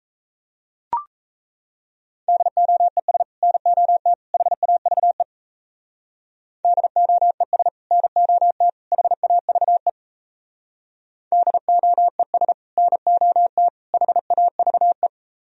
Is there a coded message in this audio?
Yes: Morse code from 2.3 to 15.1 s, heard as a keyed tone, about 37 wpm, 700 Hz. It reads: DOES NOT HAVE DOES NOT HAVE DOES NOT HAVE